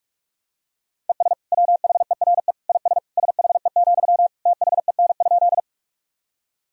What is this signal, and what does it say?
Morse code, heard as a keyed tone, about 43 wpm, 700 Hz. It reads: ES WHERE IS SHE, THEN?
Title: Sentences from Top 200 Words and Abbreviations 45wpm